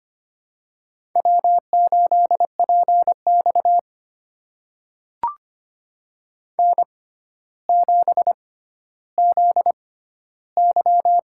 Morse code, heard as a keyed tone, about 25 wpm, spaced as for 13 wpm, 700 Hz. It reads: W8PX N7ZY
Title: USA Call Signs - 4 Characters - 6x Character Spacing - 25wpm